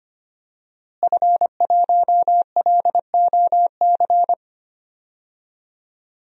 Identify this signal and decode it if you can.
Morse code, heard as a keyed tone, about 25 wpm, 700 Hz. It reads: F1LOC
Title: DXCC-100 Call Signs - 5 Characters - Repeated 3x - 25wpm